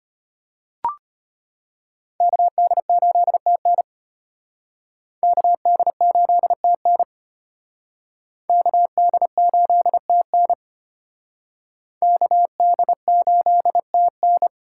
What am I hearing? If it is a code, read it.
Morse code, heard as a keyed tone, about 34 wpm, 700 Hz. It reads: KD8TN KD8TN KD8TN KD8TN